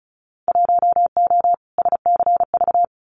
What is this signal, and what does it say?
Morse code, heard as a keyed tone, about 34 wpm, 700 Hz. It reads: 1O SCV
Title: Winter Field Day Exchanges 35wpm